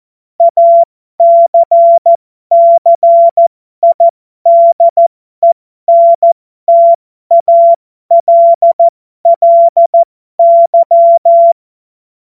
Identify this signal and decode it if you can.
Morse code, heard as a keyed tone, about 14 wpm, spaced as for 12 wpm, 680 Hz. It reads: ACCIDENTALLY
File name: Morsekurs / Witze